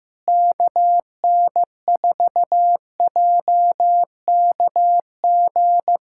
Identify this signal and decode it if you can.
Morse code, heard as a keyed tone, about 15 wpm, 700 Hz. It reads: KN4JKG
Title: USA Callsign DE Callsign 15wpm